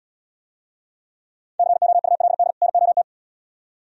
Morse code, heard as a keyed tone, about 75 wpm, 700 Hz. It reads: COULD NOT